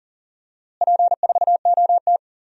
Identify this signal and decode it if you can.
Morse code, heard as a keyed tone, about 37 wpm, 700 Hz. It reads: P4OT
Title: DXCC-100 Call Signs - 4 Characters - Repeated 2x - 40wpm